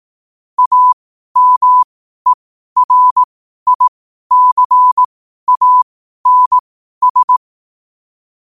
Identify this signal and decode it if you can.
Morse code, heard as a keyed tone, about 18 wpm, spaced as for 12 wpm, 1000 Hz. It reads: AMERICANS
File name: qotdfiles / 12wpm